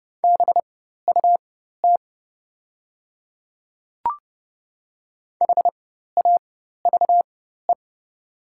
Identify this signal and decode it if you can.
Morse code, heard as a keyed tone, about 30 wpm, spaced as for 14 wpm, 700 Hz. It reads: BUT HAVE
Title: 4x Character Spacing - Top 100 Words - 30wpm